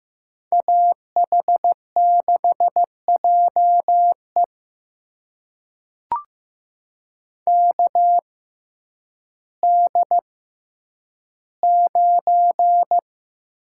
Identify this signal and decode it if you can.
Morse code, heard as a keyed tone, about 15 wpm, spaced as for 9 wpm, 700 Hz. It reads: AH6JE KD9
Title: USA Call Signs - 5 Characters - 6x Character Spacing - 15wpm